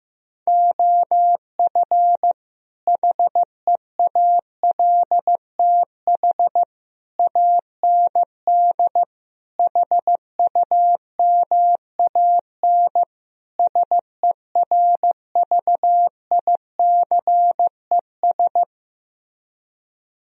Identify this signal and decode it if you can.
Morse code, heard as a keyed tone, about 15 wpm, 700 Hz. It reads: OF HEALTH AND HUMAN SERVICES
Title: Speed-Racing - Sets of 5 Words 15wpm